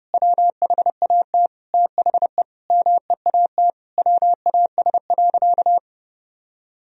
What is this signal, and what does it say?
Morse code, heard as a keyed tone, about 30 wpm, 700 Hz. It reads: WHAT THE MEAT WAS.
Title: Sentences from Top 1000 Words - Review Part 3 - 30wpm